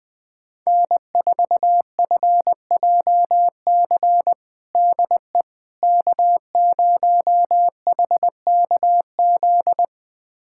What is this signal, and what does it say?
Morse code, heard as a keyed tone, about 21 wpm, 700 Hz. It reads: N4FJC DE K0HKZ
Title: USA Callsign DE Callsign - 5 Characters - 20wpm